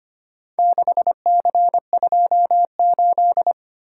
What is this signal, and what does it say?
Morse code, heard as a keyed tone, about 25 wpm, 700 Hz. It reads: 6C28
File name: practice / 25wpm / Four Letter-Number Combinations - 25wpm